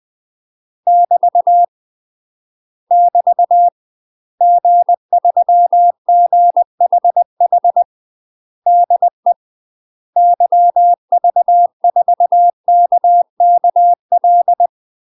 Morse code, heard as a keyed tone, about 20 wpm, 700 Hz. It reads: <BT> <BT> G3GHH DE YV4KKL
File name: UK MASTER QSOs TRAINING / 20WPM / UK MASTER QSOs 0072